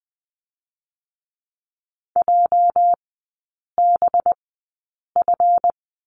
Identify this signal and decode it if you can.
Morse code, heard as a keyed tone, about 19 wpm, spaced as for 8 wpm, 700 Hz. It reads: JBF
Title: 218 Lesson 35e - Number 0 - Three Characters 20wpm